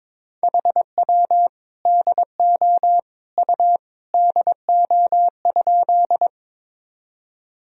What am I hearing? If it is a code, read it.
Morse code, heard as a keyed tone, about 22 wpm, 700 Hz. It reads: HW DO U DO?
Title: Sentences from Top 100 Words and Abbreviations 22wpm